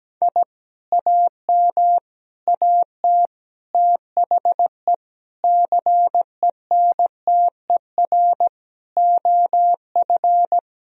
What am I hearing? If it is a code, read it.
Morse code, heard as a keyed tone, about 17 wpm, 700 Hz. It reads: I AM AT THE CENTER OF
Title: Sentences from Top 1000 Words - Review Part 2 - 17wpm